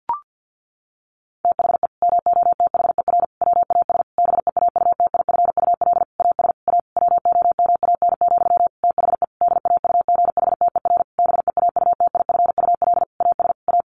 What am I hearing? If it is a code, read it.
Morse code, heard as a keyed tone, about 50 wpm, 700 Hz. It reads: THE MOTHER WAS BEAUTIFUL AS A WOMAN, THE DAUGHTER BEAUTIFUL AS A
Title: Sentences from Top 1500 Words 50wpm